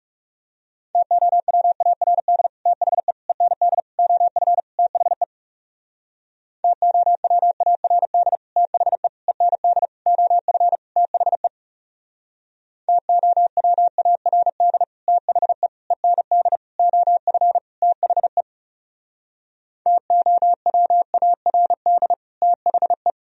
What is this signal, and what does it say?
Morse code, heard as a keyed tone, about 37 wpm, 700 Hz. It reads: TOWARD THE END OF THE TOWARD THE END OF THE TOWARD THE END OF THE TOWARD THE
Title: Speed-Racing - Sets of 5 Words 30wpm